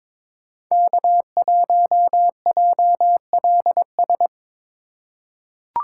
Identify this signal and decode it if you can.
Morse code, heard as a keyed tone, about 22 wpm, 700 Hz. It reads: K1JLS
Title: USA Call Signs - 5 Characters - 2x Character Spacing - 22wpm